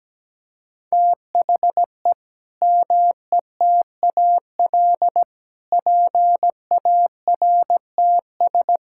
Morse code, heard as a keyed tone, about 17 wpm, 700 Hz. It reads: THE METAL PARTS
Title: Sentences from Top 1000 Words - Review Part 6 - 17wpm